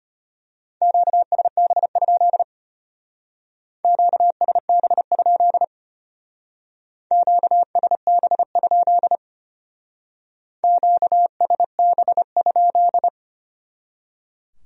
Morse code, heard as a keyed tone, about 32 wpm, 700 Hz. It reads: QSB? QSB? QSB? QSB?